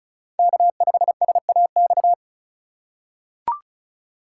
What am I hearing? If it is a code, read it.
Morse code, heard as a keyed tone, about 34 wpm, 700 Hz. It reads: K5SAX
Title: USA Call Signs - 5 Characters - 3x Character Spacing - 35wpm